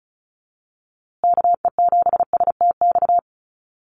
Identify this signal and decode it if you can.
Morse code, heard as a keyed tone, about 34 wpm, 700 Hz. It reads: KE7STX
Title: USA Call Signs Repeated 2x 35wpm